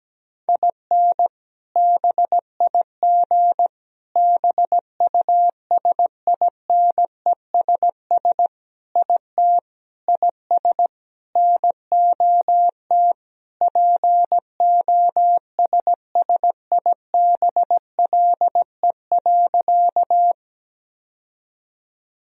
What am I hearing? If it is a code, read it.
Morse code, heard as a keyed tone, about 17 wpm, 700 Hz. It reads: IN BIG BUSINESS IT IS NOT POSSIBLE.
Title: Sentences from Top 1000 Words - Review Part 1 - 17wpm